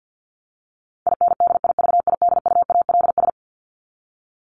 Morse code, heard as a keyed tone, about 50 wpm, 700 Hz. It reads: INDIVIDUALS
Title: ES:S Suffix - Top 100 Words with Base Words 50wpm